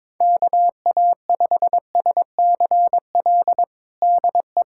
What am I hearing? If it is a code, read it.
Morse code, heard as a keyed tone, about 22 wpm, 700 Hz. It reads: KA5SCL DE
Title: USA Callsign DE Callsign - 6 Characters - 22wpm